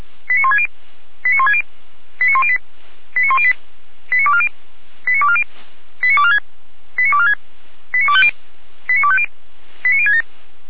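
Five-tone selective call (ZVEI), heard as quick runs of five tones, tones of about 70 ms. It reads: 89160 89160 89108 89108 89240 89240 89246 89246 89250 89250 89067
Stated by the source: Received by SDR.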